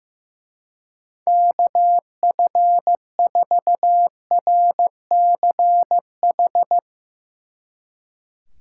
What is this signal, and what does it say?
Morse code, heard as a keyed tone, about 16 wpm, 700 Hz. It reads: KF4RCH
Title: Speed Racing - USA 2x3 Call Signs 15wpm